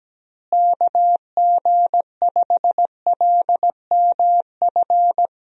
Morse code, heard as a keyed tone, about 17 wpm, 700 Hz. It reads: KG5LMF
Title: USA Call Signs - 6 Characters - Repeated 2x - 17wpm